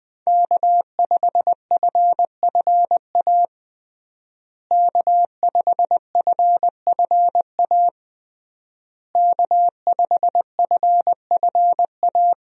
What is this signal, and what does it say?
Morse code, heard as a keyed tone, about 20 wpm, 700 Hz. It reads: K5FFA K5FFA K5FFA
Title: USA Call Signs - 5 Characters - Repeated 3x - 20wpm